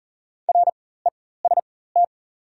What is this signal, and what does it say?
Morse code, heard as a keyed tone, about 37 wpm, spaced as for 18 wpm, 700 Hz. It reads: REST
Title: 4x Character Spacing - Top 500 Words - 40wpm